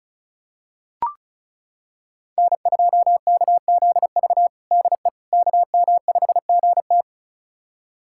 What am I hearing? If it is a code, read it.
Morse code, heard as a keyed tone, about 36 wpm, 700 Hz. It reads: N2KZV DE KM5GT
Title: USA Callsign DE Callsign - 5 Characters - 35wpm